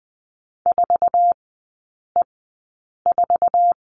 Morse code, heard as a keyed tone, about 20 wpm, spaced as for 8 wpm, 700 Hz. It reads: 4E4
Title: 047 Lesson 9e - Number 4 - Three Characters 20wpm